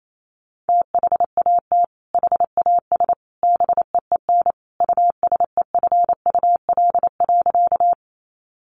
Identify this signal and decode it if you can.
Morse code, heard as a keyed tone, about 28 wpm, 700 Hz. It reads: THAT HAS BEEN USEFUL.
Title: Sentences from Top 700 Words 28wpm